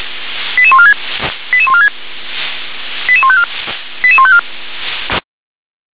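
Five-tone selective call (ZVEI), heard as quick runs of five tones, tones of about 70 ms. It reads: 88146 88146 88154 88154
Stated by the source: ZVEI five-tone selective call received by SDR.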